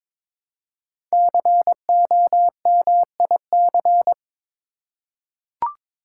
Morse code, heard as a keyed tone, about 22 wpm, 700 Hz. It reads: COMIC
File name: 5 Letter Words - First 3 Letters Given - 22wpm